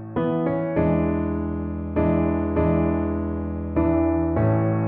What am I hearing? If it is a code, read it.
There is no code.